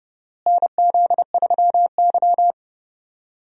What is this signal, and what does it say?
Morse code, heard as a keyed tone, about 30 wpm, 700 Hz. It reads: NZ3Y
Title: USA Call Signs - 4 Characters - Repeated 2x - 30wpm